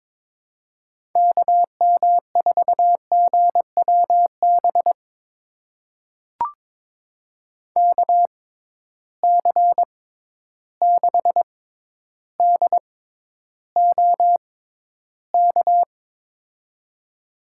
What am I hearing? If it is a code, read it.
Morse code, heard as a keyed tone, about 22 wpm, spaced as for 11 wpm, 700 Hz. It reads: KM4GWB KC6DOK
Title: USA Call Signs - 6 Characters - 6x Character Spacing - 22wpm